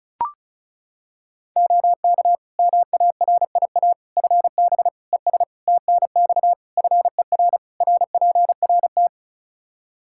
Morse code, heard as a keyed tone, about 36 wpm, 700 Hz. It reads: OK MARIA FB ES TNX FER RPRT
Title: Practice QSO - Exchange 2 - Info - 35wpm